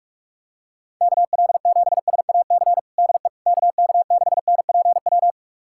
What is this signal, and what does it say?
Morse code, heard as a keyed tone, about 41 wpm, 700 Hz. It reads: KL7SAC DE KK6NPW
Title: USA Callsign DE Callsign - 6 Characters - 45wpm